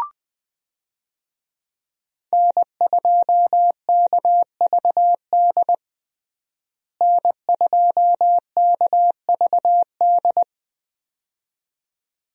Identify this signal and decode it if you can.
Morse code, heard as a keyed tone, about 20 wpm, 700 Hz. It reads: N2KVD N2KVD
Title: USA Call Signs - 5 Characters - Repeated 2x - 20wpm